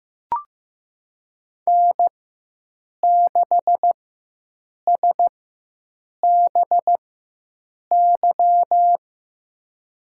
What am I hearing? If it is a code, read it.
Morse code, heard as a keyed tone, about 15 wpm, spaced as for 7 wpm, 700 Hz. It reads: N6SBY
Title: USA Call Signs - 5 Characters - 4x Character Spacing - 15wpm